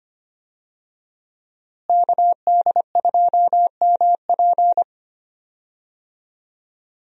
Morse code, heard as a keyed tone, about 25 wpm, 700 Hz. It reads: KD2MP
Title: USA Call Signs - 5 Characters - Repeated 2x - 25wpm